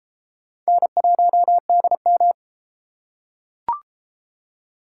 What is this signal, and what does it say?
Morse code, heard as a keyed tone, about 32 wpm, 700 Hz. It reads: N1DM